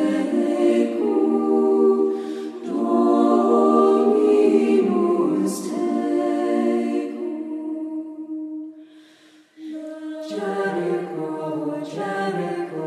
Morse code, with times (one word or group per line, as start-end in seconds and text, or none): none